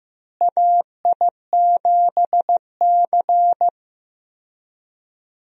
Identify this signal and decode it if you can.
Morse code, heard as a keyed tone, about 15 wpm, 700 Hz. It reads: AI7C
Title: USA Call Signs - 4 Characters - Repeated 2x - 15wpm